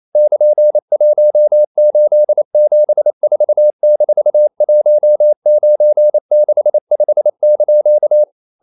Morse code, heard as a keyed tone, about 28 wpm, 600 Hz. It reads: <KN>1874-1965)
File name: CW / Practice / EnglishQuotations0228